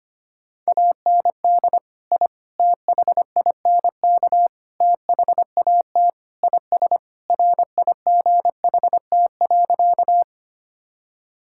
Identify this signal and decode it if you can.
Morse code, heard as a keyed tone, about 25 wpm, 700 Hz. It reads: AND I THINK THAT IS RIGHT.
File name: practice / 25wpm / Sentences with 4-Word Phrases - 25wpm